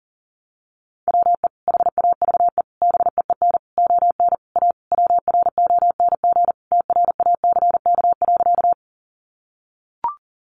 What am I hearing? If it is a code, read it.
Morse code, heard as a keyed tone, about 40 wpm, 700 Hz. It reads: WE HAVE BEEN ON A WRONG TRACK.
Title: Sentences from Top 1100 Words 40wpm